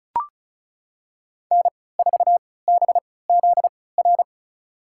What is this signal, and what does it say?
Morse code, heard as a keyed tone, about 34 wpm, spaced as for 20 wpm, 700 Hz. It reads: N4BZR
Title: USA Call Signs - 5 Characters - 3x Character Spacing - 35wpm